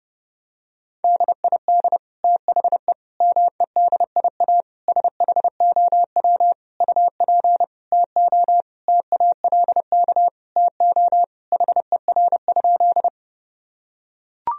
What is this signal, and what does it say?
Morse code, heard as a keyed tone, about 30 wpm, 700 Hz. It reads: DID THE MEDIA SHOW UP TO TALK TO HER?
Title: Sentences from Top 1000 Words - Review Part 4 - 30wpm